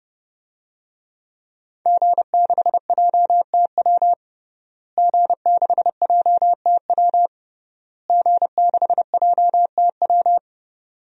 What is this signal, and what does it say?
Morse code, heard as a keyed tone, about 30 wpm, 700 Hz. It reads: G6JTW G6JTW G6JTW
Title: DXCC-100 Call Signs - 5 Characters - Repeated 3x - 30wpm